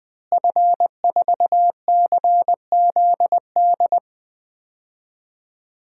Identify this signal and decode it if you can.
Morse code, heard as a keyed tone, about 21 wpm, 700 Hz. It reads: F4CZD
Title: DXCC-100 Call Signs - 5 Characters - Repeated 2x - 20wpm